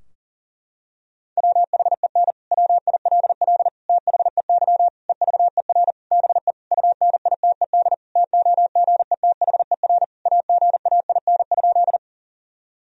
Morse code, heard as a keyed tone, about 37 wpm, 700 Hz. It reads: WHEN WILL THEY EVER BE UNITED TOGETHER AGAIN?
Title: Sentences from Top 1000 Words - Review Part 4 - 40wpm